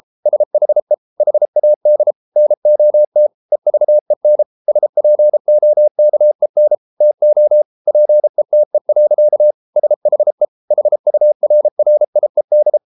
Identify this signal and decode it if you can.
Morse code, heard as a keyed tone, about 33 wpm, 600 Hz. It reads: SHE HAD NOT EVEN SPOKEN TO PETE. SHE HURRIED